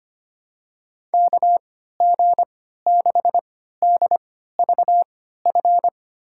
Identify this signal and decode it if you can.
Morse code, heard as a keyed tone, about 25 wpm, spaced as for 14 wpm, 700 Hz. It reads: KG6DVF
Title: USA Call Signs - 3x Character Spacing - 25wpm